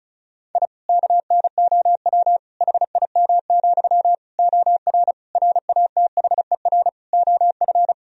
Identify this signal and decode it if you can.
Morse code, heard as a keyed tone, about 36 wpm, 700 Hz. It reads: I KNOW HIM, OR RATHER OF
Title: Sentences from Top 1000 Words - Review Part 4 - 35wpm